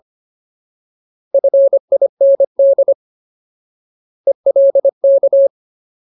Morse code, heard as a keyed tone, about 25 wpm, 550 Hz. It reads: FIND ELK
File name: Ultimate Word List / 25WPM Ultimate